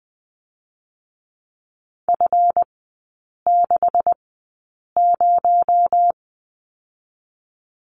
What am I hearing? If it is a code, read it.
Morse code, heard as a keyed tone, about 20 wpm, 700 Hz. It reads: F 6 0
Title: 218 Lesson 35e - Number 0 - Three Characters 20wpm